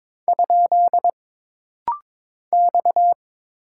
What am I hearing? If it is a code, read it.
Morse code, heard as a keyed tone, about 22 wpm, 700 Hz. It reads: ? X
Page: This rapid-fire timing is geared to push your ICR (Instant Character Recognition) skills.